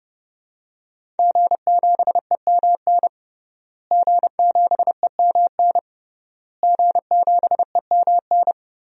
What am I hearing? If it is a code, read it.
Morse code, heard as a keyed tone, about 30 wpm, 700 Hz. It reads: G7EMN G7EMN G7EMN
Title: DXCC-100 Call Signs - 5 Characters - Repeated 3x - 30wpm